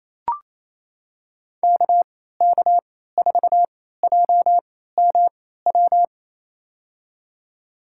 Morse code, heard as a keyed tone, about 28 wpm, spaced as for 16 wpm, 700 Hz. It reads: KK4JMW
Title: USA Call Signs - 6 Characters - 3x Character Spacing - 28wpm